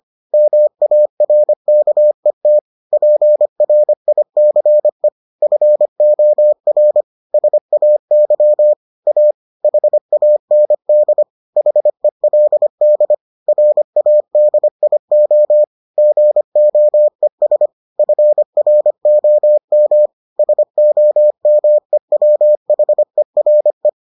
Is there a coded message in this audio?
Yes: Morse code from 0.3 to 23.9 s, heard as a keyed tone, about 25 wpm, 600 Hz. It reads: MARKET PRICE FOR SAY A HAND HELD RADIO GOES FROM SOMEWHERE